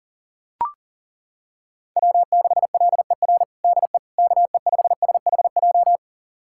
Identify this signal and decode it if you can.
Morse code, heard as a keyed tone, about 37 wpm, 700 Hz. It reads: W6LER DE KE5SHJ
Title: USA Callsign DE Callsign 40wpm